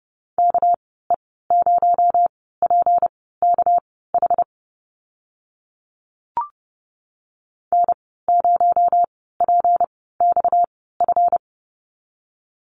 Morse code, heard as a keyed tone, about 30 wpm, spaced as for 17 wpm, 700 Hz. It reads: KE0PKH N0PXF
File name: Lesson 37g - Letter x - Call Signs 30wpm